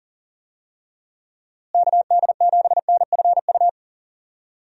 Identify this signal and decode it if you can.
Morse code, heard as a keyed tone, about 37 wpm, 700 Hz. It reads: KD7NFU
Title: USA Call Signs - 6 Characters - Repeated 2x - 40wpm